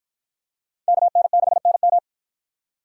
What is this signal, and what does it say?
Morse code, heard as a keyed tone, about 53 wpm, 700 Hz. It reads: KN6NM